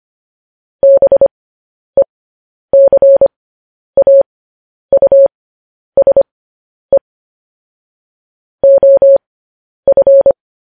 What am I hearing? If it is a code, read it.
Morse code, heard as a keyed tone, about 25 wpm, spaced as for 10 wpm, 550 Hz. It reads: BECAUSE OF
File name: podcast / News headlines-1x-25x10